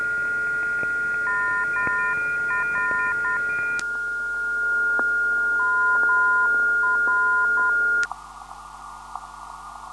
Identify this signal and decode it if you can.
Morse code, heard as a keyed tone, about 10 wpm, 1020 Hz, with noise. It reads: MR MR
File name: ndb zvok / MR-334khz